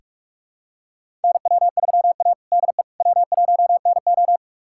Morse code, heard as a keyed tone, about 46 wpm, 700 Hz. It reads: NW3A DE W1NO
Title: USA Callsign DE Callsign - 4 Characters - 45wpm